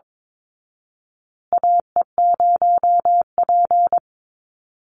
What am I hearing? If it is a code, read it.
Morse code, heard as a keyed tone, about 22 wpm, 700 Hz. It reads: AE0P
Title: Call Signs Repeated 3x - 22wpm